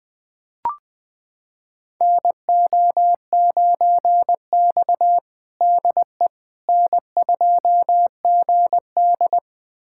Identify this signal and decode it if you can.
Morse code, heard as a keyed tone, about 20 wpm, 700 Hz. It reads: NO9X DE N2GD
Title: USA Callsign DE Callsign - 4 Characters - 20wpm